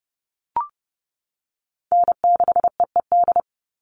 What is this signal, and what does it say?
Morse code, heard as a keyed tone, about 32 wpm, 700 Hz. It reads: N6EED
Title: Call Signs - 30wpm